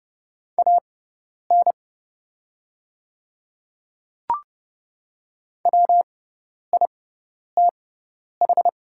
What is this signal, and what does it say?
Morse code, heard as a keyed tone, about 29 wpm, spaced as for 10 wpm, 700 Hz. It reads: AN WITH
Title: 6x Character Spacing - Top 100 Words - 30wpm